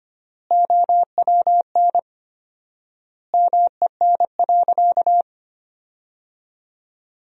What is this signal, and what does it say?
Morse code, heard as a keyed tone, about 25 wpm, 700 Hz. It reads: OWN MEN.
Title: 4x Word-Spacing - Sentences from Top 300 Words - 25wpm